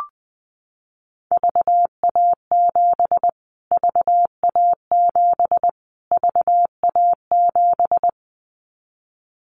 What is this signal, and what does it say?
Morse code, heard as a keyed tone, about 20 wpm, 700 Hz. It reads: VA7 VA7 VA7
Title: Callsign Prefixes for Top 25 DXCC - Repeated 3x - 20wpm